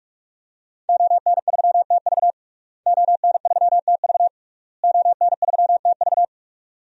Morse code, heard as a keyed tone, about 41 wpm, 700 Hz. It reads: ON3TV ON3TV ON3TV
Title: DXCC-100 Call Signs - 5 Characters - Repeated 3x - 45wpm